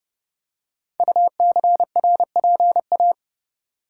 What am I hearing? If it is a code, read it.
Morse code, heard as a keyed tone, about 28 wpm, 700 Hz. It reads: UCRPA
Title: Code Groups - 5 Letter:Numbers - 30wpm